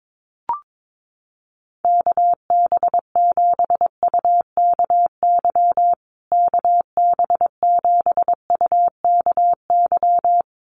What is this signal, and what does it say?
Morse code, heard as a keyed tone, about 22 wpm, 700 Hz. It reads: KB7UKY KB7UKY
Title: USA Call Signs Repeated 2x 22wpm